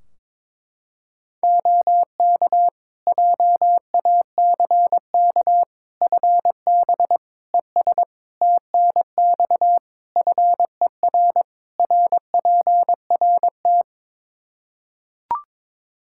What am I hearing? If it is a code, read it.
Morse code, heard as a keyed tone, about 22 wpm, 700 Hz. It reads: OK JACK FB ES TNX FER RPRT
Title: Practice QSO - Exchange 2 - Info - 22wpm